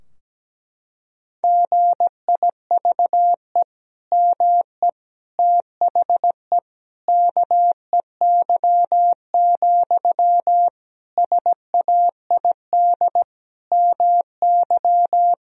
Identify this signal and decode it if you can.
Morse code, heard as a keyed tone, about 17 wpm, 700 Hz. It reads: GIVE ME THE KEY, SAID MY
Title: Sentences from Top 1000 Words - Review Part 4 - 17wpm